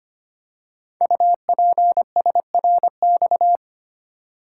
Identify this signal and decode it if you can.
Morse code, heard as a keyed tone, about 25 wpm, 700 Hz. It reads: UPSRX
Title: Code Groups - 5 Letters - 25wpm